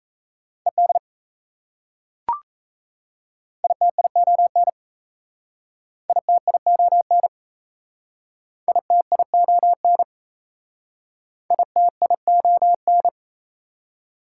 Morse code, heard as a keyed tone, about 37 wpm, 690 Hz. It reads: ED ITION ITION ITION ITION